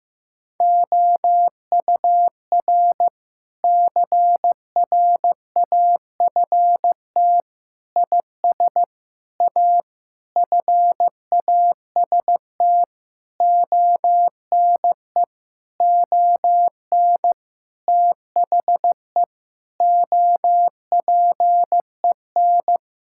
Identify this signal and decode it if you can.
Morse code, heard as a keyed tone, about 15 wpm, 700 Hz. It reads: OUR CRAFT IS A FAST ONE ON THE OPEN
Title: Sentences from Top 1000 Words - Review Part 6 - 15wpm